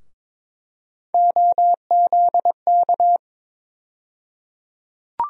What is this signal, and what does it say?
Morse code, heard as a keyed tone, about 22 wpm, 700 Hz. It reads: OZK
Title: USA Call Sign Postfixes - 3 Letters 22wpm